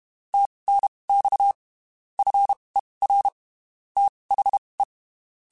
Morse code, heard as a keyed tone, about 32 wpm, spaced as for 23 wpm, 800 Hz. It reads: TNX FER THE
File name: (spaced) / QSOs Spaced-mp3s / kc7haa @30 sp